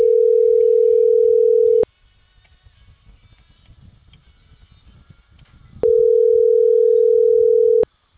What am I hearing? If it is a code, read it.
Ringback tone (North America), 440 + 480 Hz.